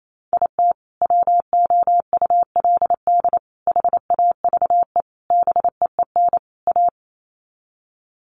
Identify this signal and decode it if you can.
Morse code, heard as a keyed tone, about 28 wpm, 700 Hz. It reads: IT WOULD HAVE BEEN A
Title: Sets of 5 Words Encore 28wpm